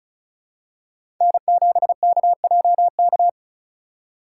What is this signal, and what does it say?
Morse code, heard as a keyed tone, about 36 wpm, 700 Hz. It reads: N7KJK
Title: USA Call Signs - 5 Characters - 3x Character Spacing - 35wpm